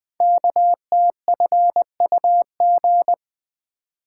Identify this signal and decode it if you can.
Morse code, heard as a keyed tone, about 20 wpm, 700 Hz. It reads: KTFUG
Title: Code Groups - 5 Letters - 20wpm